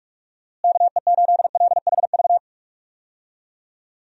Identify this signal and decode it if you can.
Morse code, heard as a keyed tone, about 43 wpm, 700 Hz. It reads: KE8LHV
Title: USA Call Signs - 6 Characters - Repeated 3x - 45wpm